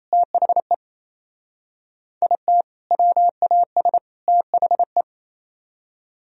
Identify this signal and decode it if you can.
Morse code, heard as a keyed tone, about 35 wpm, 700 Hz. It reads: THE IT WAS THE
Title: Speed-Racing - Sets of 3 Words 28wpm